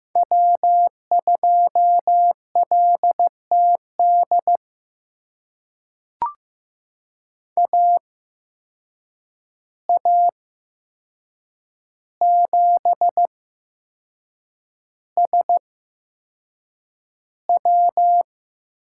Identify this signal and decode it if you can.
Morse code, heard as a keyed tone, about 15 wpm, spaced as for 6 wpm, 700 Hz. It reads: W2LTD AA7SW
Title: USA Call Signs - 5 Characters - 8x Character Spacing - 15wpm